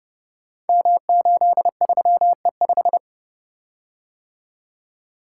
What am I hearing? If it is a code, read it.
Morse code, heard as a keyed tone, about 30 wpm, 700 Hz. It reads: M83E5
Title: Code Groups - 5 Letter:Numbers - 30wpm